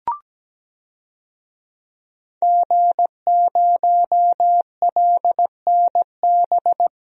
Morse code, heard as a keyed tone, about 17 wpm, 700 Hz. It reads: G0LNB
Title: DXCC-100 Call Signs - 5 Characters - Repeated 3x - 17wpm